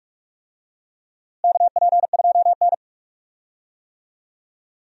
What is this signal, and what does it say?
Morse code, heard as a keyed tone, about 46 wpm, 700 Hz. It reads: KP2N
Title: DXCC-100 Call Signs - 4 Characters - Repeated 3x - 45wpm